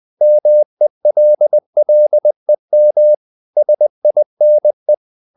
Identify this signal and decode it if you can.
Morse code, heard as a keyed tone, about 20 wpm, 600 Hz. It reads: MELLEM SINE
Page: Ferie 20 wpm